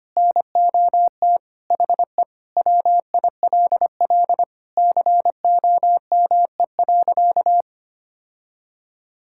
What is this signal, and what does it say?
Morse code, heard as a keyed tone, about 25 wpm, 700 Hz. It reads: NOT HE WILL COME.
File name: practice / 25wpm / Sentences with 3-Word Phrases - 25wpm